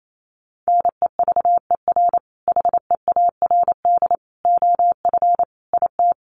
Morse code, heard as a keyed tone, about 28 wpm, 700 Hz. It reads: NEVER HEARD OF IT
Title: Sentences from Top 400 Words 28wpm